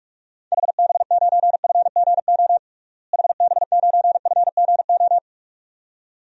Morse code, heard as a keyed tone, about 43 wpm, 700 Hz. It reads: HB9FGO HB9FGO
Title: DXCC-100 Call Signs - 6 Characters - Repeated 2x - 45wpm